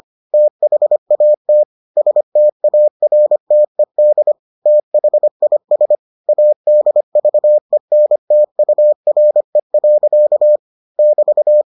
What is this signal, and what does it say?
Morse code, heard as a keyed tone, about 25 wpm, 600 Hz. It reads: THAT STARTED THIS ADVENTURE. <BT>